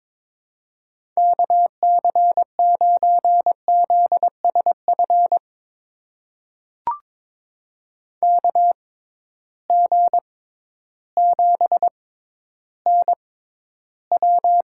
Morse code, heard as a keyed tone, about 22 wpm, spaced as for 12 wpm, 700 Hz. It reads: KC9ZSF KG7NW
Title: USA Call Signs - 6x Character Spacing - 22wpm